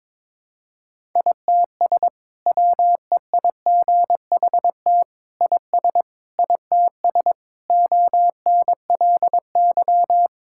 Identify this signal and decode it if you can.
Morse code, heard as a keyed tone, about 22 wpm, 700 Hz. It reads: ITS WEIGHT IS ITS ONLY
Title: Sentences from Top 1000 Words - Review Part 6 - 22wpm